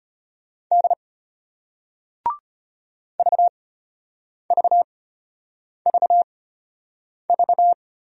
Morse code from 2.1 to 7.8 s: VVVV

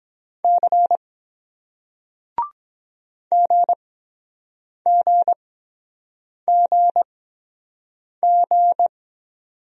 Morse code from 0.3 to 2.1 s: C

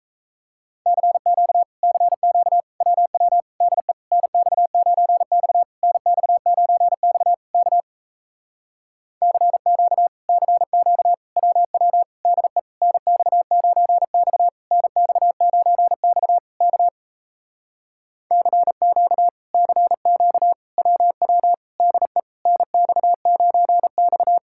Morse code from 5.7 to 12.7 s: NX9X K CQ CQ WW DE

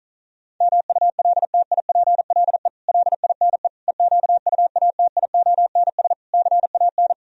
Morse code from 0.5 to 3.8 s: MULTIPLE LINE

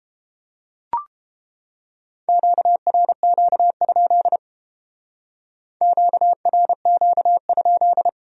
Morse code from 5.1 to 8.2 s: QRQ?

